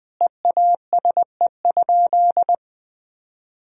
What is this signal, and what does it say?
Morse code, heard as a keyed tone, about 20 wpm, 700 Hz. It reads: EASE?
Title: Sentences from Top 1000 Words - Review Part 1 - 20wpm